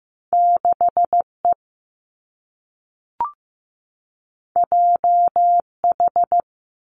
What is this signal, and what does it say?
Morse code, heard as a keyed tone, about 15 wpm, 700 Hz. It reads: 6 E J H
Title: Two Challenging Letter-Number Combinations 15wpm